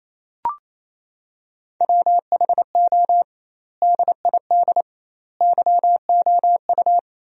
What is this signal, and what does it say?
Morse code, heard as a keyed tone, about 28 wpm, 700 Hz. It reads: WHO DID YOU
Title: 2x Word-Spacing - Sentences from Top 300 Words - 28wpm